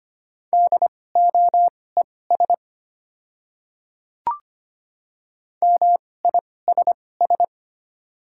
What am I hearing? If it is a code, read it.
Morse code, heard as a keyed tone, about 25 wpm, spaced as for 18 wpm, 700 Hz. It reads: DOES MISS
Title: 2x Character Spacing - Top 500 Words - 25wpm